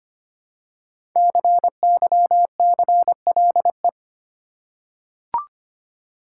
Morse code, heard as a keyed tone, about 25 wpm, 700 Hz. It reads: CYCLE
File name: 5 Letter Words - First 3 Letters Given - 25wpm